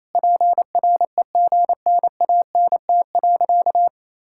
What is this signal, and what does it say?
Morse code, heard as a keyed tone, about 27 wpm, 700 Hz. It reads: PREGNANT.
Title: Sentences from Top 1000 Words - Review Part 6 - 28wpm